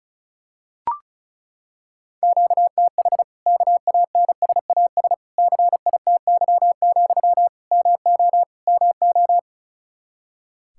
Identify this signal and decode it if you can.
Morse code, heard as a keyed tone, about 36 wpm, 700 Hz. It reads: QTH KANSAS CITY, MO MO
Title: Practice QSO - Exchange 1 - Beginning - 35wpm